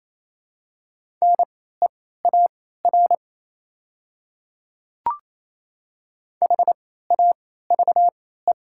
Morse code, heard as a keyed tone, about 28 wpm, spaced as for 16 wpm, 700 Hz. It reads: NEAR HAVE
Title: 3x Character Spacing - Top 200 Words - 28wpm